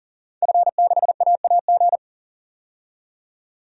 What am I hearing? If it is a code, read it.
Morse code, heard as a keyed tone, about 40 wpm, 700 Hz. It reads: F6AAG
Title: DXCC-100 Call Signs - 5 Characters - 40wpm